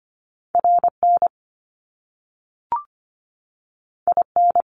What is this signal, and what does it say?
Morse code, heard as a keyed tone, about 25 wpm, 700 Hz. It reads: RN IN
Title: Top Two Letter Combinations in Words 25wpm